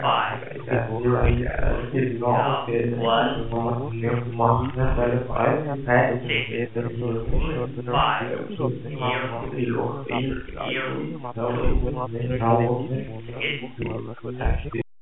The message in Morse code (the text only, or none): none